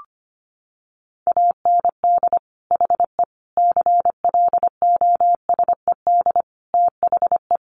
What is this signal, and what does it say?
Morse code, heard as a keyed tone, about 25 wpm, 700 Hz. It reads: AND HE CLOSED THE
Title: Sentences from Top 1500 Words 25wpm